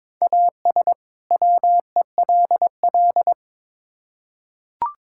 Morse code, heard as a keyed tone, about 22 wpm, 700 Hz. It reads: AS WELL
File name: Build A Phrase - 2-Word Phrases - 22wpm